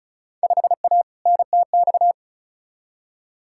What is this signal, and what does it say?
Morse code, heard as a keyed tone, about 36 wpm, 700 Hz. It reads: 5A NTX
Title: Field Day Exchanges - 35wpm